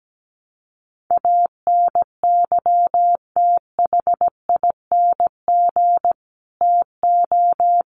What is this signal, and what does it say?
Morse code, heard as a keyed tone, about 17 wpm, 700 Hz. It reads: ANYTHING TO
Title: Sets of 5 Words Encore 17wpm